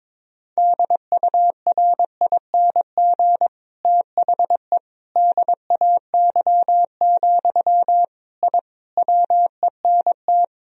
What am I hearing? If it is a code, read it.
Morse code, heard as a keyed tone, about 22 wpm, 700 Hz. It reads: DURING THE DAY, I WENT
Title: Sentences from Top 1000 Words - Review Part 2 - 22wpm